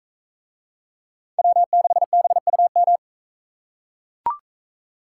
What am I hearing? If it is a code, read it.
Morse code, heard as a keyed tone, about 40 wpm, 700 Hz. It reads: W6BUM